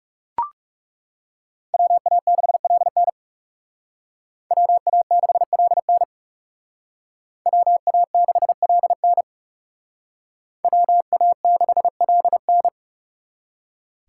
Morse code, heard as a keyed tone, about 39 wpm, 700 Hz. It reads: WA6LN WA6LN WA6LN WA6LN